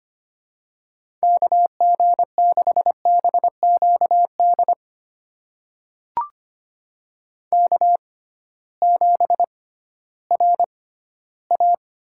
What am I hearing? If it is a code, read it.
Morse code, heard as a keyed tone, about 25 wpm, spaced as for 14 wpm, 700 Hz. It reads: KG6BQD K7RA
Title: USA Call Signs - 6x Character Spacing - 25wpm